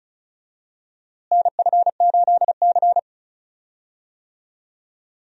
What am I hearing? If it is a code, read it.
Morse code, heard as a keyed tone, about 34 wpm, 700 Hz. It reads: NF8C
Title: USA Call Signs - 4 Characters - Repeated 3x - 35wpm